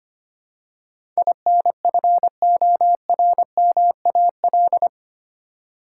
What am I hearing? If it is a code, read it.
Morse code, heard as a keyed tone, about 25 wpm, 700 Hz. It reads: INFORMAL
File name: Eight Letter Words - 25wpm